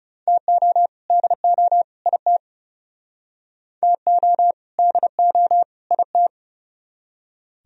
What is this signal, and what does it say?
Morse code, heard as a keyed tone, about 37 wpm, 700 Hz. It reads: TO DO IT TO DO IT
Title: Speed-Racing - Sets of 3 Words 30wpm